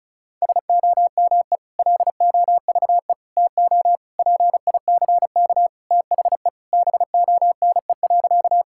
Morse code, heard as a keyed tone, about 36 wpm, 700 Hz. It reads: SOME LOVE TO PICK THE BONE.
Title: Sentences from Top 1600 Words 35wpm